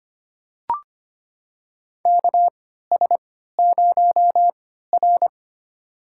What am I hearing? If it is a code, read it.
Morse code, heard as a keyed tone, about 25 wpm, spaced as for 14 wpm, 700 Hz. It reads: KS0R
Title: USA Call Signs - 4 Characters - 3x Character Spacing - 25wpm